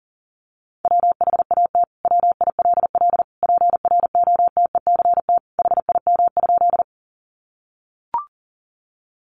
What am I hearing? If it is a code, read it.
Morse code, heard as a keyed tone, about 40 wpm, 700 Hz. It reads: WHAT WILL PROTECT HIM?
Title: Sentences from Top 900 Words 40wpm